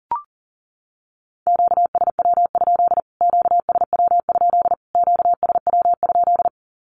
Morse code, heard as a keyed tone, about 40 wpm, 700 Hz. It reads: QSW? QSW? QSW?
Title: Q-Codes Repeated 3x 40wpm